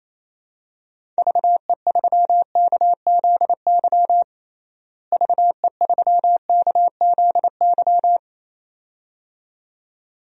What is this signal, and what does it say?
Morse code, heard as a keyed tone, about 28 wpm, 700 Hz. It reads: VE3KZY VE3KZY
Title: DXCC-100 Call Signs - 6 Characters - Repeated 2x - 28wpm